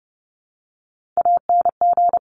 Morse code, heard as a keyed tone, about 30 wpm, 700 Hz. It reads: ANG
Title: Top Three Letter Combinations in Words 30wpm